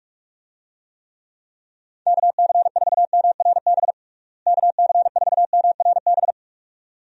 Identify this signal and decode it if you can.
Morse code, heard as a keyed tone, about 46 wpm, 700 Hz. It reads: KC4MRB KC4MRB
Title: USA Call Signs - 6 Characters - Repeated 2x - 45wpm